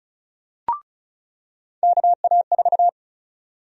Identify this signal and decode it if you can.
Morse code, heard as a keyed tone, about 34 wpm, 700 Hz. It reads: KA4
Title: USA Call Sign Prefixes - 35wpm